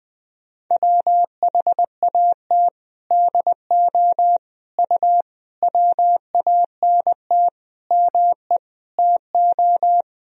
Morse code, heard as a keyed tone, about 21 wpm, 700 Hz. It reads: WHAT DO U WANT ME TO
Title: Sentences from Top 200 Words and Abbreviations 20wpm